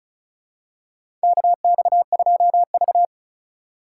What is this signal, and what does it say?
Morse code, heard as a keyed tone, about 36 wpm, 690 Hz. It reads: KX2V
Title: USA Call Signs - 3x Character Spacing - 35wpm